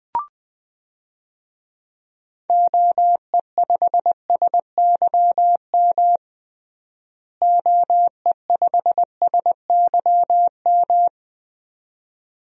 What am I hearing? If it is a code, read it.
Morse code, heard as a keyed tone, about 21 wpm, 700 Hz. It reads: OE5SYM OE5SYM
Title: DXCC-100 Call Signs - 6 Characters - Repeated 2x - 20wpm